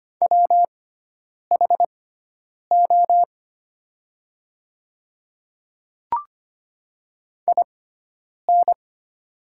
Morse code, heard as a keyed tone, about 25 wpm, spaced as for 9 wpm, 700 Hz. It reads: WHO IN
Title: 6x Character Spacing - Top 200 Words - 25wpm